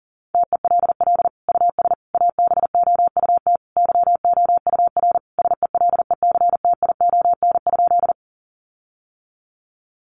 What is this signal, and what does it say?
Morse code, heard as a keyed tone, about 40 wpm, 700 Hz. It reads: TELL US ABOUT YOUR SELECTION?
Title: Sentences from Top 1400 Words 40wpm